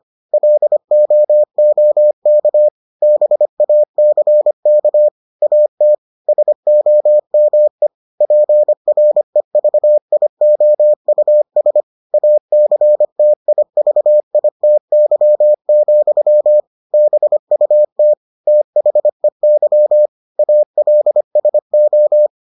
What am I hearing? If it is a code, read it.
Morse code, heard as a keyed tone, about 25 wpm, 600 Hz. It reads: LOOK BACK AT SOME PREVIOUS ACTIVITY, BUT THEY ALSO